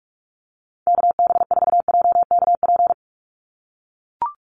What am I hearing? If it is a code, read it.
Morse code, heard as a keyed tone, about 44 wpm, 700 Hz. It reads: KB4JKP
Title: USA Call Signs 45wpm